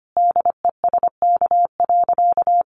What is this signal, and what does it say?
Morse code, heard as a keyed tone, about 25 wpm, 700 Hz. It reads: DESK.
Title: Easy Phrases 25wpm